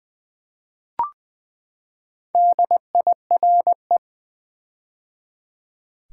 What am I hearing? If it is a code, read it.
Morse code, heard as a keyed tone, about 20 wpm, 700 Hz. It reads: DIRE
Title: Build A Word - 4 Letters At A Time - 8-Letter Words - 20wpm